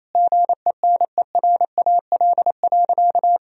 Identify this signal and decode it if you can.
Morse code, heard as a keyed tone, about 28 wpm, 700 Hz. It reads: GENERAL.
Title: Sentences from Top 1000 Words - Review Part 3 - 28wpm